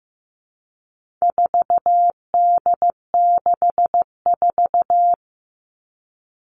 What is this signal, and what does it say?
Morse code, heard as a keyed tone, about 15 wpm, 700 Hz. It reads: 4D64
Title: All Cut Numbers - 4-Digit 15wpm